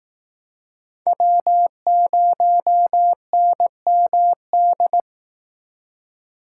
Morse code, heard as a keyed tone, about 19 wpm, 700 Hz. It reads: W0NMD